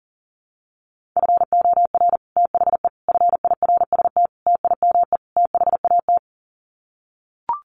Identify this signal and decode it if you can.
Morse code, heard as a keyed tone, about 40 wpm, 700 Hz. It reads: FOR THE FIRST TIME THAT
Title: Sets of 5 Words Encore 40wpm